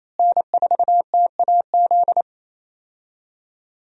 Morse code, heard as a keyed tone, about 27 wpm, 700 Hz. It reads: N4TAZ
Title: USA Call Signs - 5 Characters - Repeated 2x - 28wpm